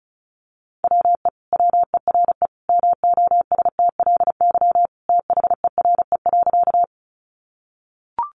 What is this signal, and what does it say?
Morse code, heard as a keyed tone, about 34 wpm, 700 Hz. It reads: WE WERE MOSTLY THERE.
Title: Sentences from Top 1300 Words 35wpm